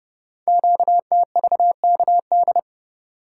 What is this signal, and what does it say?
Morse code, heard as a keyed tone, about 30 wpm, 700 Hz. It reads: QTVKD
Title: Code Groups - 5 Letter:Numbers - 30wpm